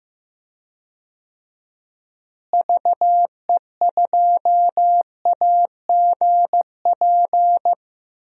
Morse code, heard as a keyed tone, about 15 wpm, 700 Hz. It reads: VE2AGP
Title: DXCC-100 Call Signs - 6 Characters - Repeated 2x - 15wpm